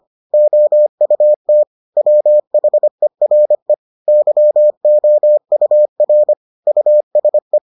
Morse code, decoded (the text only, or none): OUT WHERE YOUR USE